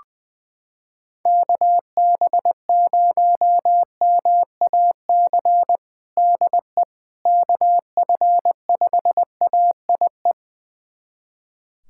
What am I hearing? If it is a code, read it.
Morse code, heard as a keyed tone, about 20 wpm, 700 Hz. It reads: KB0MAC DE KF5AIE
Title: USA Callsign DE Callsign - 6 Characters - 20wpm